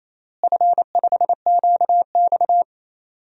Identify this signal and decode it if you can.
Morse code, heard as a keyed tone, about 28 wpm, 700 Hz. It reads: F5QX
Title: DXCC-100 Call Signs - 4 Characters - Repeated 2x - 28wpm